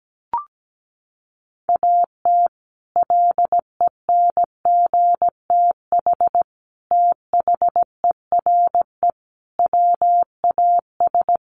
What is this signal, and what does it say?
Morse code, heard as a keyed tone, about 17 wpm, 700 Hz. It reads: AT LENGTH THERE WAS
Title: Sentences from Top 1000 Words 17wpm